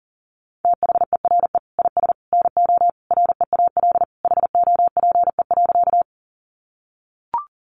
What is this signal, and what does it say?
Morse code, heard as a keyed tone, about 37 wpm, 700 Hz. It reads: THERE IS NO REAL HOPE.
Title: Sentences from Top 1000 Words 40wpm